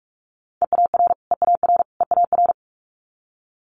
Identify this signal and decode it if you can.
Morse code, heard as a keyed tone, about 46 wpm, 700 Hz. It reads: EAR EAR EAR
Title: Multi-letter Phonemes Repeated 3x 45wpm